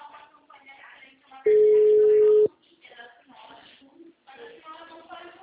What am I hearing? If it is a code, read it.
Ringback tone (Europe, 425 Hz), with speech.